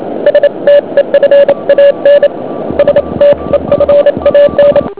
Morse code, heard as a keyed tone, about 28 wpm, 560 Hz. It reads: STEFAN STEFAN
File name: ARCHIV / QRP2008 / OK2BEI OK1HMP